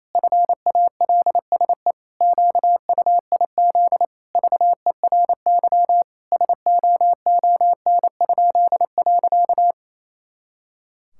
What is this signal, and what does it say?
Morse code, heard as a keyed tone, about 28 wpm, 700 Hz. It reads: FALSE QUIZ VERY SOON?.
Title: Pangram Challenge v2 - 28wpm